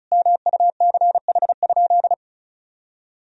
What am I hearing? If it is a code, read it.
Morse code, heard as a keyed tone, about 34 wpm, 700 Hz. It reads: MUCH?
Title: Sentences from Top 1000 Words - Review Part 2 - 35wpm